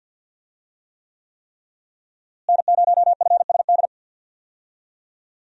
Morse code, heard as a keyed tone, about 52 wpm, 700 Hz. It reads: N0FSD
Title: USA Call Signs - 5 Characters - Repeated 3x - 50wpm